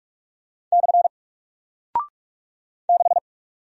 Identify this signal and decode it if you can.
Morse code, heard as a keyed tone, about 44 wpm, 700 Hz. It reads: / 6
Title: Single Character (Letter:Number:Punctuation:Prosign) Alt 45wpm